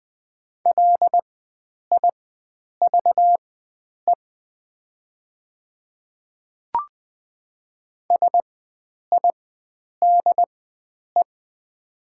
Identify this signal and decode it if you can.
Morse code, heard as a keyed tone, about 20 wpm, spaced as for 9 wpm, 690 Hz. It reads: LIVE SIDE
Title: 4x Character Spacing - Top 300 Words - 20wpm